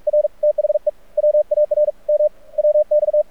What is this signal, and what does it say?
Morse code, heard as a keyed tone, about 46 wpm, 600 Hz. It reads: R THE WARM WX
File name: archived / CRSnet